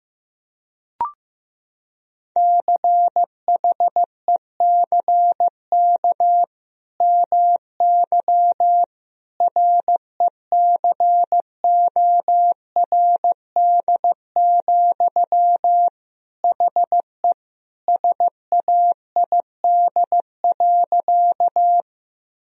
Morse code, heard as a keyed tone, about 15 wpm, 700 Hz. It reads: CHECK MY RECORD, HE SAID.
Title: Sentences from Top 1000 Words - Review Part 3 - 15wpm